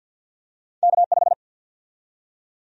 Morse code, heard as a keyed tone, about 48 wpm, 700 Hz. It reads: K5
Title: USA Call Sign Prefixes - Repeated 2x - 50wpm